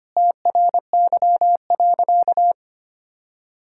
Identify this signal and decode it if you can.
Morse code, heard as a keyed tone, about 25 wpm, 700 Hz. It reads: TRY.
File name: practice / 25wpm / Sentences with 2-Word Phrases - 25wpm